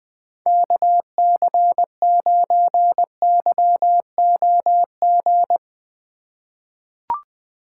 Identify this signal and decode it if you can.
Morse code, heard as a keyed tone, about 20 wpm, 700 Hz. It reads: KC9YOG
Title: USA Call Signs - 6 Characters - 4x Character Spacing - 20wpm